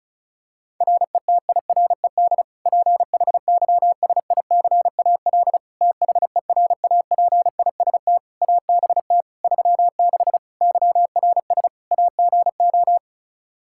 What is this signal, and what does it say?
Morse code, heard as a keyed tone, about 36 wpm, 700 Hz. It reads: RETIRED PHYSICAL THERAPIST ABT 36 YRS AGO
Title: Practice QSO - Exchange 3 - More Info - 35wpm